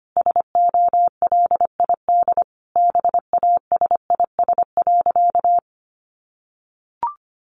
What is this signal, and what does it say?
Morse code, heard as a keyed tone, about 25 wpm, 700 Hz. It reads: SOLID BASIS.
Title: Sentences from Top 1500 Words 25wpm